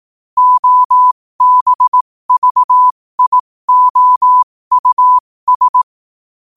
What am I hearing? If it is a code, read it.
Morse code, heard as a keyed tone, about 18 wpm, spaced as for 15 wpm, 1000 Hz. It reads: OBVIOUS
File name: qotdfiles / 15wpm